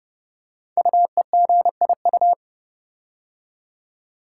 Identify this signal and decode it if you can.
Morse code, heard as a keyed tone, about 30 wpm, 700 Hz. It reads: UEGIU
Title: Code Groups - 5 Letters - 30wpm